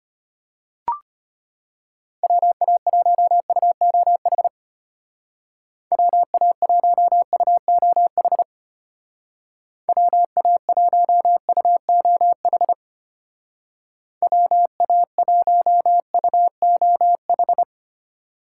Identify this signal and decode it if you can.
Morse code, heard as a keyed tone, about 33 wpm, 700 Hz. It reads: WA1UOH WA1UOH WA1UOH WA1UOH